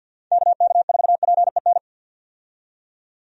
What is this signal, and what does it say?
Morse code, heard as a keyed tone, about 51 wpm, 700 Hz. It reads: KK4PEN
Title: USA Call Signs - 6 Characters - 50wpm